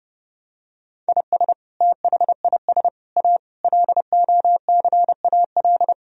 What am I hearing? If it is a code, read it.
Morse code, heard as a keyed tone, about 30 wpm, 700 Hz. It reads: IS THIS A LOCAL
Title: Sentences from Top 1000 Words - Review Part 3 - 30wpm